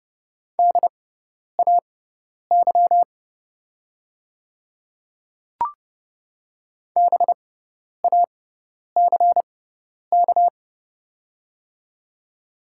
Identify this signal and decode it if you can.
Morse code, heard as a keyed tone, about 30 wpm, spaced as for 10 wpm, 690 Hz. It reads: DAY BACK
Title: 6x Character Spacing - Top 200 Words - 30wpm